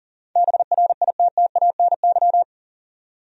Morse code, heard as a keyed tone, about 38 wpm, 700 Hz. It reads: BRITTANY
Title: USA Common Names - 40wpm